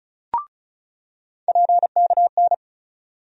Morse code, heard as a keyed tone, about 34 wpm, 700 Hz. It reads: PKN
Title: USA Call Sign Postfixes - 3 Letters 35wpm